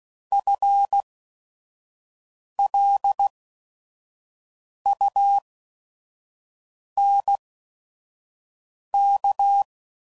Morse code, heard as a keyed tone, about 16 wpm, spaced as for 5 wpm, 800 Hz. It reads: FLUNK